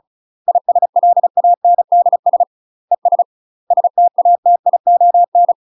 Morse code, heard as a keyed tone, about 36 wpm, 700 Hz. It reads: ISLANDS ES STATION
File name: UK MASTER QSOs TRAINING / 35WPM / UK MASTER QSOs 0020